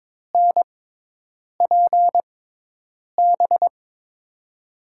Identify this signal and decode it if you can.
Morse code, heard as a keyed tone, about 21 wpm, spaced as for 8 wpm, 700 Hz. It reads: NPB
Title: USA Call Signs - 6 Characters - 6x Character Spacing - 22wpm